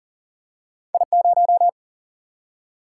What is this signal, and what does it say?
Morse code, heard as a keyed tone, about 40 wpm, 700 Hz. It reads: I0
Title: Single-Letter Single-Number 40wpm